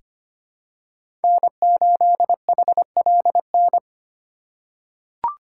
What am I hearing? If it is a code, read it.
Morse code, heard as a keyed tone, about 25 wpm, 700 Hz. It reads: N8HLN
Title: USA Call Signs - 4x Character Spacing - 25wpm